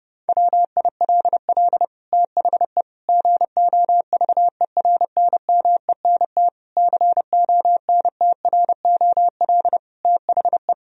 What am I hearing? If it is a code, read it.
Morse code, heard as a keyed tone, about 32 wpm, 700 Hz. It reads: WILL THE GOVERNMENT CONTROL THE
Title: Sentences from Top 1000 Words - Review Part 3 - 30wpm